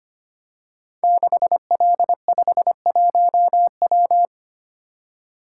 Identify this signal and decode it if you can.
Morse code, heard as a keyed tone, about 25 wpm, 700 Hz. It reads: 6L51W
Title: Code Groups - 5 Letter:Numbers - 25wpm